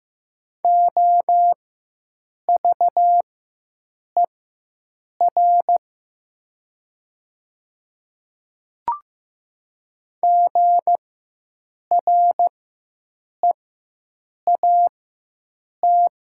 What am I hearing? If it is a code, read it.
Morse code, heard as a keyed tone, about 15 wpm, spaced as for 7 wpm, 700 Hz. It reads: OVER GREAT
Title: 4x Character Spacing - Top 200 Words - 15wpm